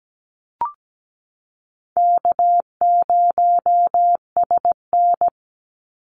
Morse code, heard as a keyed tone, about 17 wpm, 700 Hz. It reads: K0SN
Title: Call Signs Repeated 3x - 17wpm